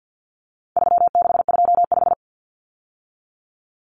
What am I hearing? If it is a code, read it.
Morse code, heard as a keyed tone, about 50 wpm, 700 Hz. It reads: 3625
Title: Common Cut Numbers - 4-Digit 50wpm